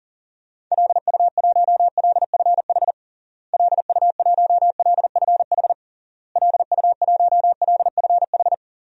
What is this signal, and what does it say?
Morse code, heard as a keyed tone, about 37 wpm, 700 Hz. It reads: LU1LFH LU1LFH LU1LFH
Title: DXCC-100 Call Signs - 6 Characters - Repeated 3x - 40wpm